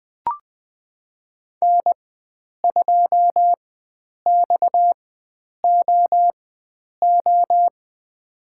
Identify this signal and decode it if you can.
Morse code, heard as a keyed tone, about 21 wpm, spaced as for 9 wpm, 700 Hz. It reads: N2XOO
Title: USA Call Signs - 5 Characters - 4x Character Spacing - 20wpm